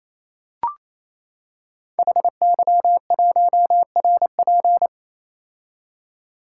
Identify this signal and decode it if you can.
Morse code, heard as a keyed tone, about 29 wpm, 700 Hz. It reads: HY1RP
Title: Code Groups - 5 Letter:Numbers - 28wpm